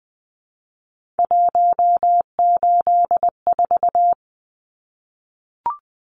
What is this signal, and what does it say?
Morse code, heard as a keyed tone, about 20 wpm, 700 Hz. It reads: 184
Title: Numbers - 3-Digit 20wpm